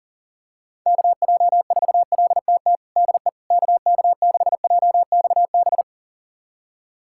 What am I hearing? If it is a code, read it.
Morse code, heard as a keyed tone, about 40 wpm, 700 Hz. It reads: KJ4LTT DE KK6JXB
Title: USA Callsign DE Callsign 40wpm